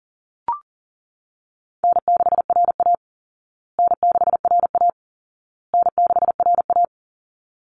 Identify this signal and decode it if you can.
Morse code, heard as a keyed tone, about 40 wpm, 700 Hz. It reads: N6RA N6RA N6RA
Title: Easy Call Signs Repeated 3x - 40wpm